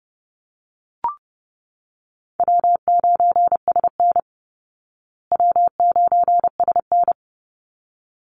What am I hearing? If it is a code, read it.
Morse code, heard as a keyed tone, about 30 wpm, 700 Hz. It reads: W9SN W9SN
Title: Easy Call Signs Repeated 2x - 30wpm